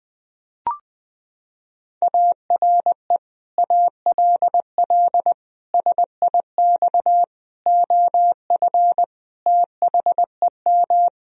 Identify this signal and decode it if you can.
Morse code, heard as a keyed tone, about 20 wpm, 700 Hz. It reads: ARE ALL SIX OF THEM
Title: Sentences from Top 1000 Words - Review Part 2 - 20wpm